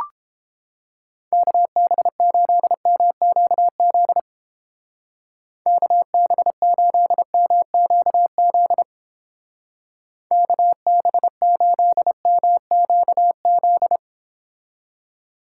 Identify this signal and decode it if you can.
Morse code, heard as a keyed tone, about 29 wpm, 700 Hz. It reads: KB8MQZ KB8MQZ KB8MQZ